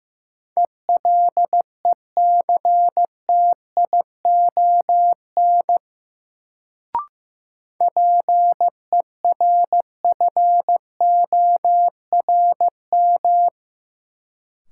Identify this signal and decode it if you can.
Morse code, heard as a keyed tone, about 15 wpm, 700 Hz. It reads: ELECTION PERFORM
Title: Sentences from Top 1500 Words - New Word Drill Alt - 15wpm